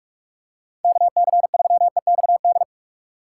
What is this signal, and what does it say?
Morse code, heard as a keyed tone, about 44 wpm, 700 Hz. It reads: KC3EXD
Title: USA Call Signs - 6 Characters - 4x Character Spacing - 45wpm